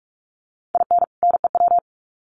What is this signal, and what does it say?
Morse code, heard as a keyed tone, about 44 wpm, 700 Hz. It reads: IN NEW
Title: Sets of 2 Words 45wpm